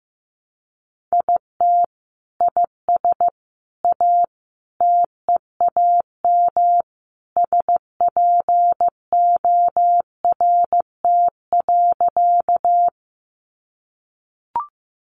Morse code, heard as a keyed tone, about 15 wpm, 700 Hz. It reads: IT IS A TEAM SPORT.
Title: Sentences from Top 1500 Words 15wpm